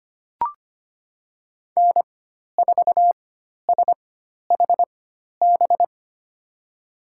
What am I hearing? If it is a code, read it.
Morse code, heard as a keyed tone, about 25 wpm, spaced as for 12 wpm, 700 Hz. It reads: N4SHB
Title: USA Call Signs - 4x Character Spacing - 25wpm